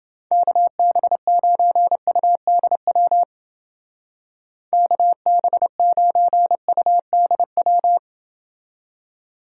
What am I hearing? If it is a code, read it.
Morse code, heard as a keyed tone, about 29 wpm, 700 Hz. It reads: KB9UDW KB9UDW